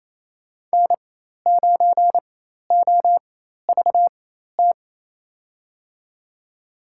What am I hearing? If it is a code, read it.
Morse code, heard as a keyed tone, about 28 wpm, spaced as for 13 wpm, 700 Hz. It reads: N9OVT
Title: USA Call Signs - 4x Character Spacing - 28wpm